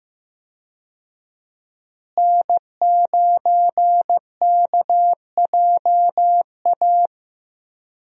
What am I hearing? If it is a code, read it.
Morse code, heard as a keyed tone, about 15 wpm, 700 Hz. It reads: N9KJA
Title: USA Call Signs - 5 Characters - Repeated 2x - 15wpm